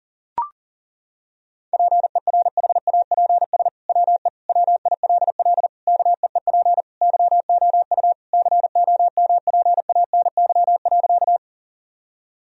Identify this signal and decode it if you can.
Morse code, heard as a keyed tone, about 40 wpm, 700 Hz. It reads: PERHAPS WE WILL KEEP YOU COMPANY.
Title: Sentences from Top 1000 Words - Review Part 3 - 40wpm